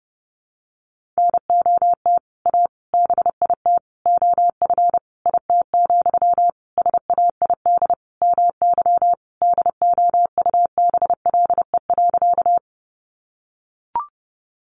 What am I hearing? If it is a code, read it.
Morse code, heard as a keyed tone, about 30 wpm, 700 Hz. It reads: NOT A BIT OF IT, SAID MY DOUBLE.
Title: Sentences from Top 1400 Words 30wpm